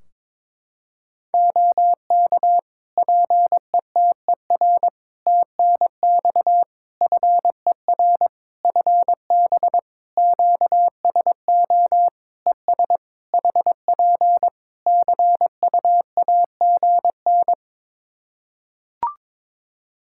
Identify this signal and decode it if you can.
Morse code, heard as a keyed tone, about 22 wpm, 700 Hz. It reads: OK PETER TNX FER FB QSO ES HP CUAGN
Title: Practice QSO - Exchange 4 - Ending - 22wpm